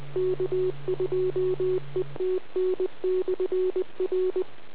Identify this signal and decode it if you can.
Morse code, heard as a keyed tone, about 20 wpm, 370 Hz. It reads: K2ETN/R